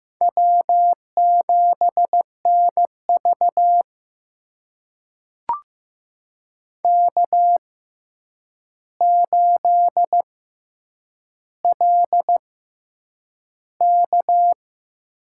Morse code, heard as a keyed tone, about 15 wpm, spaced as for 8 wpm, 700 Hz. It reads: W7NV K8LK
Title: USA Call Signs - 6x Character Spacing - 15wpm